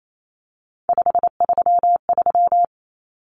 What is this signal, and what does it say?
Morse code, heard as a keyed tone, about 28 wpm, 700 Hz. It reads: 533
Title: Numbers - 3-Digit 28wpm